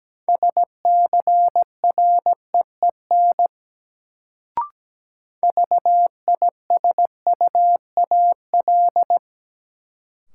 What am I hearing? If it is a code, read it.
Morse code, heard as a keyed tone, about 17 wpm, 700 Hz. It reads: SCREEN VISUAL
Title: Sentences from Top 1000 Words - Review Part 6 - New Word Drill - 17wpm